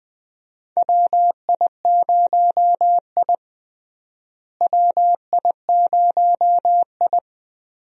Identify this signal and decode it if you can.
Morse code, heard as a keyed tone, about 21 wpm, 700 Hz. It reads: WI0I WI0I
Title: USA Call Signs - 4 Characters - Repeated 2x - 20wpm